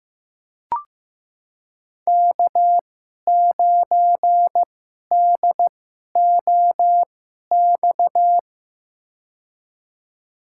Morse code, heard as a keyed tone, about 15 wpm, spaced as for 11 wpm, 700 Hz. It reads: K9DOX
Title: USA Call Signs - 2x Character Spacing - 15wpm